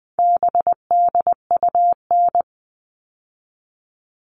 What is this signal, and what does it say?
Morse code, heard as a keyed tone, about 20 wpm, 700 Hz. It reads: BDUN
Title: All Cut Numbers - 4-Digit 20wpm